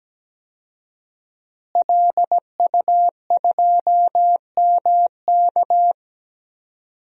Morse code, heard as a keyed tone, about 17 wpm, 700 Hz. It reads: LU2MK
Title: DXCC-100 Call Signs - 5 Characters - Repeated 3x - 17wpm